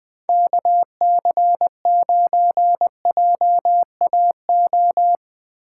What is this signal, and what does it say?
Morse code, heard as a keyed tone, about 20 wpm, 700 Hz. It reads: KC9JAO
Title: USA Call Signs - 6 Characters - 2x Character Spacing - 20wpm